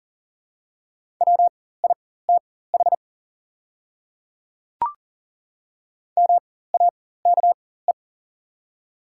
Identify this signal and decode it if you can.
Morse code, heard as a keyed tone, about 40 wpm, spaced as for 19 wpm, 700 Hz. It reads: WITH MAKE
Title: 4x Character Spacing - Top 100 Words - 40wpm